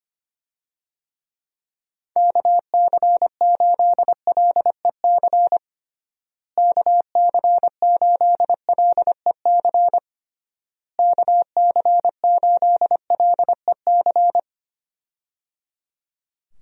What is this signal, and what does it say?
Morse code, heard as a keyed tone, about 25 wpm, 700 Hz. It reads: KC8LEC KC8LEC KC8LEC
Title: USA Call Signs - 6 Characters - Repeated 3x - 25wpm